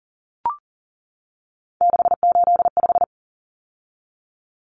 Morse code, heard as a keyed tone, about 40 wpm, 700 Hz. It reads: 685
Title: Common Cut Numbers - 3-Digit 40wpm